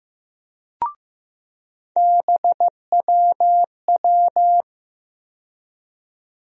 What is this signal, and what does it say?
Morse code, heard as a keyed tone, about 15 wpm, 700 Hz. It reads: BWW
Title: USA Call Sign Postfixes - 3 Letters 15wpm